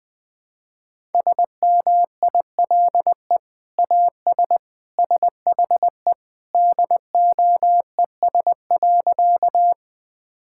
Morse code, heard as a keyed tone, about 20 wpm, 700 Hz. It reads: SMILE AS SHE DOES.
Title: Sentences with 2-Word Phrases - Spoken Beginning - 20wpm